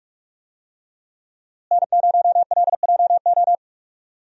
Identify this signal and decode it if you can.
Morse code, heard as a keyed tone, about 44 wpm, 700 Hz. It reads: N0LJO
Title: USA Call Signs - 5 Characters - Repeated 3x - 45wpm